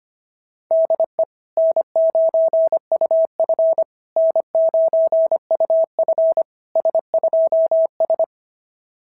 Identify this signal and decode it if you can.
Morse code, heard as a keyed tone, about 25 wpm, 690 Hz. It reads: DE N9UF N9UF S2S
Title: SOTA QSO Practice 25wpm